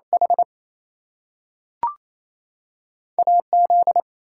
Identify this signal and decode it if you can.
Morse code, heard as a keyed tone, about 28 wpm, spaced as for 6 wpm, 700 Hz. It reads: HAZ
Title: US States and Canadian Provinces 28wpm